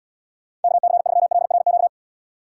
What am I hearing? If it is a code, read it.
Morse code, heard as a keyed tone, about 74 wpm, 700 Hz. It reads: KC3LRQ